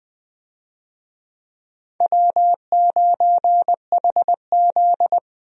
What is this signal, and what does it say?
Morse code, heard as a keyed tone, about 21 wpm, 700 Hz. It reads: W9HZ
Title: USA Call Signs - 4 Characters - 20wpm